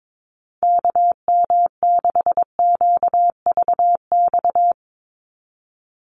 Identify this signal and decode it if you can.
Morse code, heard as a keyed tone, about 22 wpm, 700 Hz. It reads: KM6QVX
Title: USA Call Signs 22wpm